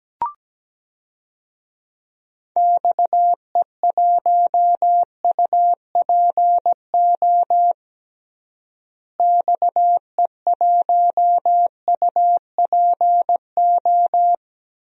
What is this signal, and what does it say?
Morse code, heard as a keyed tone, about 17 wpm, 700 Hz. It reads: XE1UPO XE1UPO
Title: DXCC-100 Call Signs - 6 Characters - Repeated 3x - 17wpm